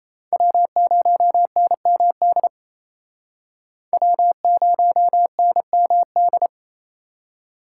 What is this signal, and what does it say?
Morse code, heard as a keyed tone, about 35 wpm, 700 Hz. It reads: W0NMD W0NMD
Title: Speed Racing - USA 1x3 Call Signs 28wpm